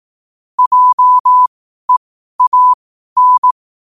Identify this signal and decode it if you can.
Morse code, heard as a keyed tone, about 18 wpm, spaced as for 12 wpm, 1000 Hz. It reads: JEAN
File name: qotdfiles / 12wpm